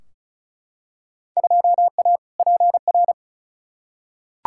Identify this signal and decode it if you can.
Morse code, heard as a keyed tone, about 34 wpm, 700 Hz. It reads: 2A PR
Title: Field Day Exchanges Encore - 35wpm